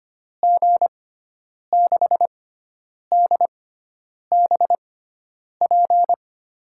Morse code, heard as a keyed tone, about 25 wpm, spaced as for 9 wpm, 700 Hz. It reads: G6DBP
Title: USA Call Signs - 6x Character Spacing - 25wpm